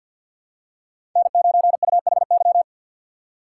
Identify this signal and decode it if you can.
Morse code, heard as a keyed tone, about 51 wpm, 700 Hz. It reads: N9UHY
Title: USA Call Signs - 5 Characters - 4x Character Spacing - 50wpm